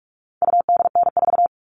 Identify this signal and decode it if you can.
Morse code, heard as a keyed tone, about 44 wpm, 700 Hz. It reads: UDN4
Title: All Cut Numbers - 4-Digit 45wpm